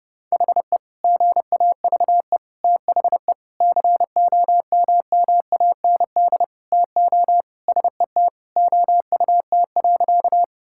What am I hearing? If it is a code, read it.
Morse code, heard as a keyed tone, about 30 wpm, 700 Hz. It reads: HE GAVE THE COMMAND TO SET OUT.
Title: Sentences from Top 1600 Words 30wpm